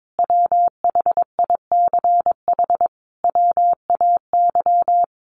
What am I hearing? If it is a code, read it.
Morse code, heard as a keyed tone, about 22 wpm, 700 Hz. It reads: WHICH WAY
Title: Sentences from Top 1100 Words 22wpm